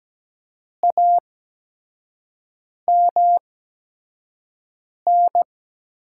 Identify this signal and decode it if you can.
Morse code, heard as a keyed tone, about 17 wpm, spaced as for 5 wpm, 700 Hz. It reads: AMN
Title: USA Call Signs - 8x Character Spacing - 17wpm